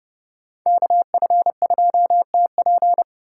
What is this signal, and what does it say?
Morse code, heard as a keyed tone, about 29 wpm, 700 Hz. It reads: KF2TP
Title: K1USN SST QSO Practice - 30wpm